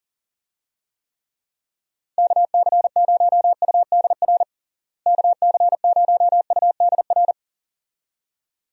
Morse code, heard as a keyed tone, about 40 wpm, 700 Hz. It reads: KC0UDR KC0UDR
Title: USA Call Signs - 6 Characters - Repeated 2x - 40wpm